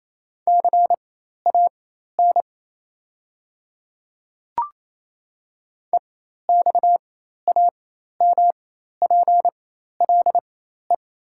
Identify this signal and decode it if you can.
Morse code, heard as a keyed tone, about 28 wpm, spaced as for 13 wpm, 700 Hz. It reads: CAN EXAMPLE
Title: 4x Character Spacing - Top 500 Words - 28wpm